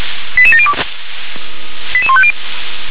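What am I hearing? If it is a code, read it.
Five-tone selective call (ZVEI), heard as quick runs of five tones, tones of about 70 ms. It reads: 88702 88150
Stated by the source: Five-tone ZVEI selective call received with an SDR.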